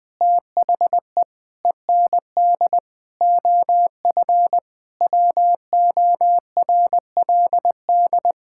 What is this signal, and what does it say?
Morse code, heard as a keyed tone, about 20 wpm, 700 Hz. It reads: THE END OF WORLD
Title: Sentences with 7-Word Phrases - 20wpm